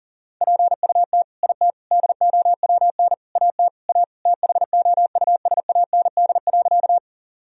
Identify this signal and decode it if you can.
Morse code, heard as a keyed tone, about 40 wpm, 700 Hz. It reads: PUT IT DOWN AT A THOUSAND.
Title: Sentences from Top 1000 Words - Review Part 3 - 40wpm